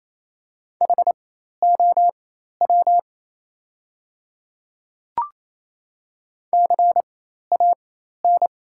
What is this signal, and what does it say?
Morse code, heard as a keyed tone, about 28 wpm, spaced as for 13 wpm, 690 Hz. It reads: HOW CAN
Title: 4x Character Spacing - Top 200 Words - 28wpm